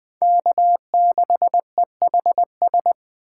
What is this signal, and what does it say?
Morse code, heard as a keyed tone, about 20 wpm, 700 Hz. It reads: K6EHS